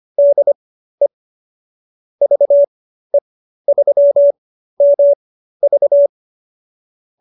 Morse code, heard as a keyed tone, about 25 wpm, spaced as for 13 wpm, 570 Hz. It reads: DE VE3MV